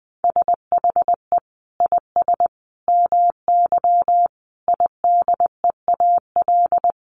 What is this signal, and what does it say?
Morse code, heard as a keyed tone, about 20 wpm, 700 Hz. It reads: SHE IS MY IDEAL
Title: Sentences from Top 1400 Words 20wpm